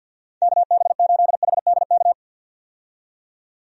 Morse code, heard as a keyed tone, about 48 wpm, 700 Hz. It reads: KB8HDK
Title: USA Call Signs - 6 Characters - 50wpm